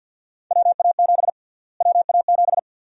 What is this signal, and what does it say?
Morse code, heard as a keyed tone, about 53 wpm, 700 Hz. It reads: WA7 WA7
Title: USA Call Sign Prefixes - Repeated 2x - 50wpm